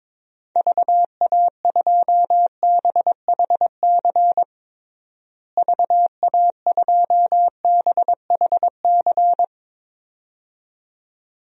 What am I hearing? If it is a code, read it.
Morse code, heard as a keyed tone, about 22 wpm, 700 Hz. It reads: VA2BHC VA2BHC
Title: DXCC-100 Call Signs - 6 Characters - Repeated 2x - 22wpm